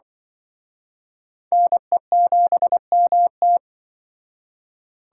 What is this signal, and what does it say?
Morse code, heard as a keyed tone, about 23 wpm, 700 Hz. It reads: NE7MT
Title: Speed Racing - USA 2x2 Call Signs 20wpm